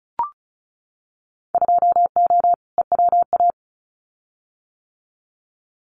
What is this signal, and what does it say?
Morse code, heard as a keyed tone, about 34 wpm, 700 Hz. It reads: 2O EWA
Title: Winter Field Day Exchanges Encore 35wpm